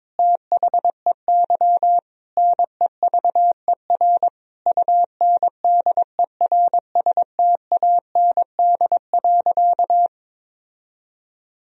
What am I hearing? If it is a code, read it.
Morse code, heard as a keyed tone, about 22 wpm, 700 Hz. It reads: THEY NEVER UNDERSTAND.
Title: Sentences from Top 1000 Words - Review Part 3 - 22wpm